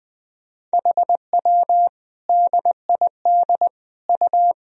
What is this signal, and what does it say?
Morse code, heard as a keyed tone, about 21 wpm, 700 Hz. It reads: HW DID U
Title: Sentences from Top 100 Words and Abbreviations 20wpm